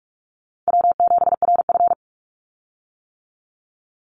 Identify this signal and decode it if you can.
Morse code, heard as a keyed tone, about 46 wpm, 700 Hz. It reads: W7RF
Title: Easy Call Signs - 45wpm